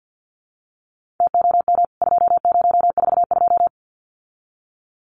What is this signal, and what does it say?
Morse code, heard as a keyed tone, about 50 wpm, 700 Hz. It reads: TOM 2042
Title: CWT Exchange 50wpm